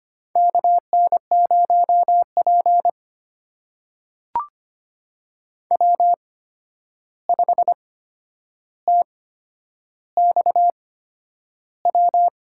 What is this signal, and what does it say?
Morse code, heard as a keyed tone, about 25 wpm, spaced as for 10 wpm, 700 Hz. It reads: KN0P W5TXW
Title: USA Call Signs - 8x Character Spacing - 25wpm